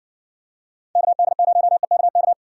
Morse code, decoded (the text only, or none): KD0EKK